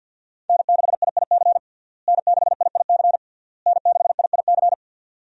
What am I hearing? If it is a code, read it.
Morse code, heard as a keyed tone, about 51 wpm, 700 Hz. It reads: N6IIC N6IIC N6IIC
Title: USA Call Signs - 5 Characters - Repeated 3x - 50wpm